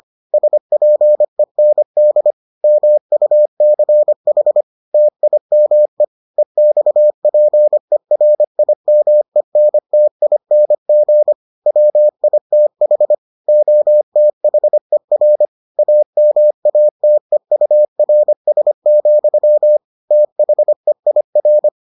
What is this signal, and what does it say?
Morse code, heard as a keyed tone, about 25 wpm, 600 Hz. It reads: SPEND MUCH TIME EXPERIMENTING WITH OTHER AMATEURS, THEIR